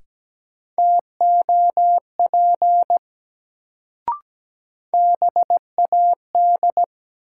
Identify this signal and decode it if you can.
Morse code, heard as a keyed tone, about 17 wpm, 700 Hz. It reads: TOP BAD
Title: Sentences from Top 700 Words - New Word Drill - 17wpm